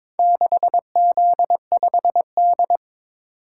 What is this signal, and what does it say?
Morse code, heard as a keyed tone, about 22 wpm, 700 Hz. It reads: 6Z5D
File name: practice / 22wpm / Four Letter-Number Combinations - 22wpm